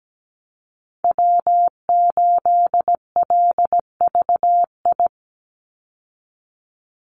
Morse code, heard as a keyed tone, about 17 wpm, 700 Hz. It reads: W8LVI
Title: Call Signs Repeated 2x - 17wpm